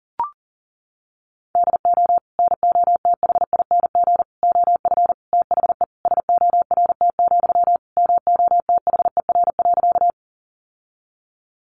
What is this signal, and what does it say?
Morse code, heard as a keyed tone, about 40 wpm, 700 Hz. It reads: DO NOTHING OF THE SORT, MOTHER.
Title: Sentences from Top 1300 Words 40wpm